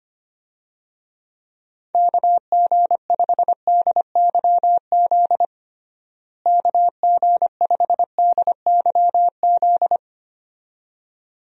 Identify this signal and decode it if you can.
Morse code, heard as a keyed tone, about 25 wpm, 700 Hz. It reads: KG5DYZ KG5DYZ
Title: USA Call Signs - 6 Characters - Repeated 2x - 25wpm